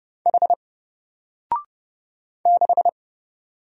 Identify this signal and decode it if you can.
Morse code, heard as a keyed tone, about 29 wpm, 700 Hz. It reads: H 6
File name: Single-Letter-Number-30wpm